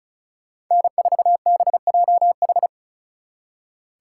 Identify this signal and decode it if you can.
Morse code, heard as a keyed tone, about 46 wpm, 700 Hz. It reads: N4BJH